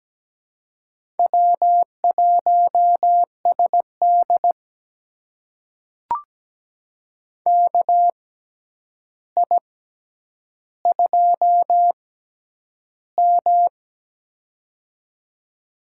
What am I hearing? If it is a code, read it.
Morse code, heard as a keyed tone, about 17 wpm, spaced as for 9 wpm, 700 Hz. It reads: W1SD KI2M
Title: USA Call Signs - 6x Character Spacing - 17wpm